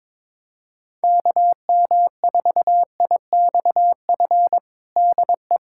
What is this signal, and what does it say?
Morse code, heard as a keyed tone, about 22 wpm, 700 Hz. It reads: KM4IXF DE
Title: USA Callsign DE Callsign 22wpm